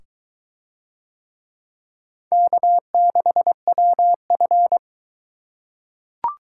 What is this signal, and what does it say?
Morse code, heard as a keyed tone, about 21 wpm, 700 Hz. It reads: K6WF